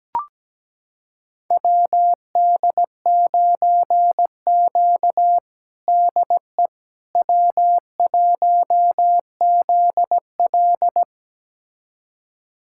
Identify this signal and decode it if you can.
Morse code, heard as a keyed tone, about 17 wpm, 700 Hz. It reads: WD9Q DE W1ZL
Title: USA Callsign DE Callsign - 4 Characters - 17wpm